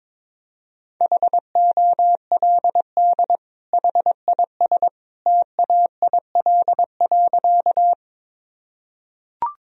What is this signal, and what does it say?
Morse code, heard as a keyed tone, about 22 wpm, 700 Hz. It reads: HOLD HIS TAIL.
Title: Sentences from Top 1000 Words - Review Part 3 - 22wpm